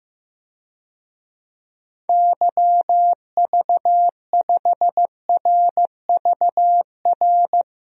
Morse code, heard as a keyed tone, about 15 wpm, 700 Hz. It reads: YV5RVR
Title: DXCC-100 Call Signs - 6 Characters - 15wpm